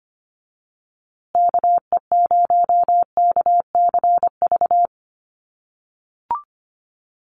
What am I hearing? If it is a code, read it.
Morse code, heard as a keyed tone, about 25 wpm, 700 Hz. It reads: KE0KCV
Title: USA Call Signs Repeated 3x 25wpm